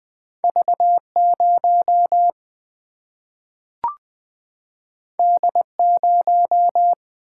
Morse code, heard as a keyed tone, about 20 wpm, 700 Hz. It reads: V0 D0
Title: Single-Letter Single-Number 20wpm